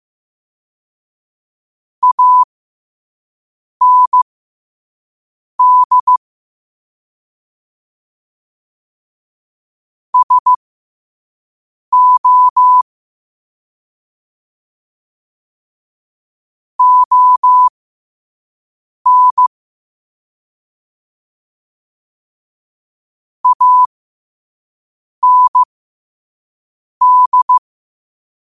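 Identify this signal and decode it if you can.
Morse code, heard as a keyed tone, about 15 wpm, spaced as for 5 wpm, 1000 Hz. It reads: AND SO ON AND
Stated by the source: Morse quote-of-the-day practice at 5 wpm.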